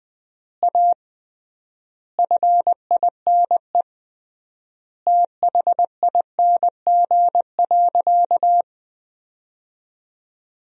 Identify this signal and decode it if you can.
Morse code, heard as a keyed tone, about 20 wpm, 700 Hz. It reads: A FINE THING.
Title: 3x Word-Spacing - Sentences from Top 500 Words - 20wpm